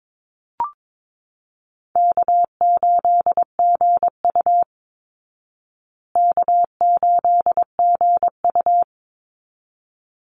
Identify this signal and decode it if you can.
Morse code, heard as a keyed tone, about 22 wpm, 690 Hz. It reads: K8GU K8GU
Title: Easy Call Signs Repeated 2x - 22wpm